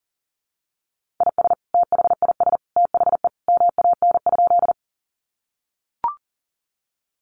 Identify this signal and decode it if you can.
Morse code, heard as a keyed tone, about 40 wpm, 700 Hz. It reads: IS THIS THE MAN?
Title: Sentences from Top 200 Words 40wpm